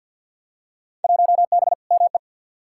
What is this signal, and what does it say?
Morse code, heard as a keyed tone, about 51 wpm, 700 Hz. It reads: 1B ME
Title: Field Day Exchanges - 50wpm